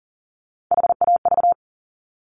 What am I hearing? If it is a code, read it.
Morse code, heard as a keyed tone, about 42 wpm, 700 Hz. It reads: HAV
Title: Top Three Letter Combinations in Words 40wpm